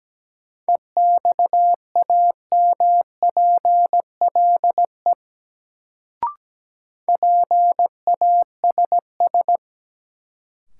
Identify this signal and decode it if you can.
Morse code, heard as a keyed tone, about 17 wpm, 700 Hz. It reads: EXAMPLE PASS
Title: Sentences from Top 400 Words - New Word Drill - 17wpm